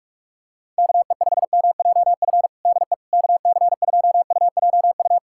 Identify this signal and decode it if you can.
Morse code, heard as a keyed tone, about 46 wpm, 700 Hz. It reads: KE5MJF DE KC2UJU
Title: USA Callsign DE Callsign - 6 Characters - 45wpm